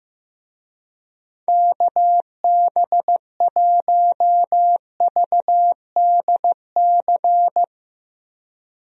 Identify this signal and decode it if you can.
Morse code, heard as a keyed tone, about 14 wpm, 700 Hz. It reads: KB1VDC
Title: USA Call Signs - 6 Characters - Repeated 3x - 15wpm